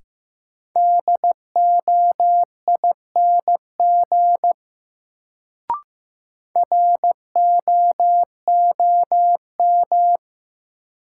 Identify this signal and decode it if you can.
Morse code, heard as a keyed tone, about 15 wpm, 700 Hz. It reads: DOING ROOM
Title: Sets of 4 Words - New Word Drill - 15wpm